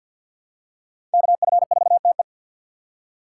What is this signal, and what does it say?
Morse code, heard as a keyed tone, about 52 wpm, 700 Hz. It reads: KF4TE
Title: USA Call Signs - 3x Character Spacing - 50wpm